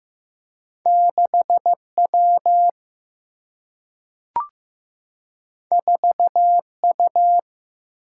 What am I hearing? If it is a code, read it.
Morse code, heard as a keyed tone, about 15 wpm, 700 Hz. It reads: 6W 4U
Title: Single-Letter Single-Number 15wpm